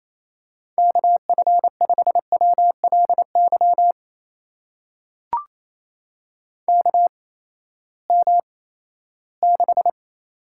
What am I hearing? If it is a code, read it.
Morse code, heard as a keyed tone, about 28 wpm, spaced as for 16 wpm, 700 Hz. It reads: KF5WLY KM6
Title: USA Call Signs - 6 Characters - 8x Character Spacing - 28wpm